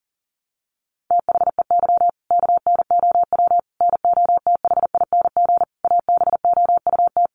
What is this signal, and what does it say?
Morse code, heard as a keyed tone, about 40 wpm, 700 Hz. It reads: THEY KNOW NOTHING ABOUT
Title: Sentences from Top 500 Words 40wpm